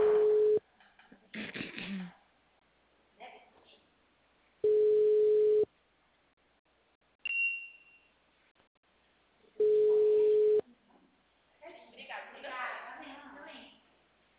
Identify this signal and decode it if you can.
Ringback tone (Europe, 425 Hz), with speech.
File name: out- -2013- - - .26699